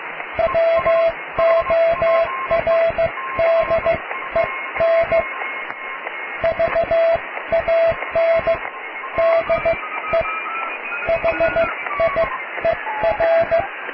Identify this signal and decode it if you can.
Morse code, heard as a keyed tone, about 15 wpm, spaced as for 13 wpm, 660 Hz, with radio noise. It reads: WORDEN VAN DE HIER
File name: cw / tekst(a)+QRM